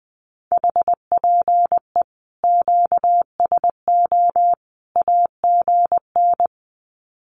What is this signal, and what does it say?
Morse code, heard as a keyed tone, about 20 wpm, 700 Hz. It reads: HPE QSO AGN
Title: Common QSO Phrases 20wpm